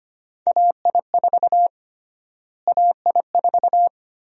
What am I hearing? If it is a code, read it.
Morse code, heard as a keyed tone, about 25 wpm, 700 Hz. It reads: AI4 AI4
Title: USA Call Sign Prefixes - Repeated 2x - 25wpm